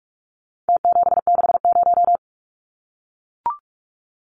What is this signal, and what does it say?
Morse code, heard as a keyed tone, about 44 wpm, 700 Hz. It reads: T760
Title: Numbers - Contest Style - 4-Digit 45wpm